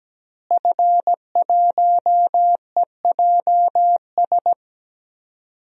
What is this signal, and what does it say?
Morse code, heard as a keyed tone, about 17 wpm, 700 Hz. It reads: F1EJS
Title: DXCC-100 Call Signs - 5 Characters - Repeated 3x - 17wpm